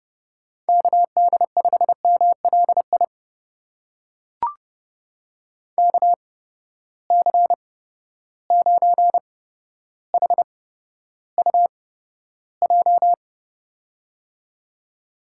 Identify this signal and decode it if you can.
Morse code, heard as a keyed tone, about 31 wpm, spaced as for 13 wpm, 700 Hz. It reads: KD5MLI KC9HUJ
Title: USA Call Signs - 6 Characters - 8x Character Spacing - 30wpm